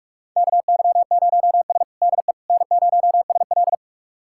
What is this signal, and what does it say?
Morse code, heard as a keyed tone, about 45 wpm, 700 Hz. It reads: KY0S DE N0SL
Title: USA Callsign DE Callsign - 4 Characters - 45wpm